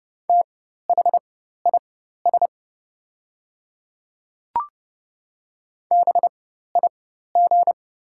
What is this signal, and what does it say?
Morse code, heard as a keyed tone, about 30 wpm, spaced as for 14 wpm, 700 Hz. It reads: THIS BIG
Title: 4x Character Spacing - Top 200 Words - 30wpm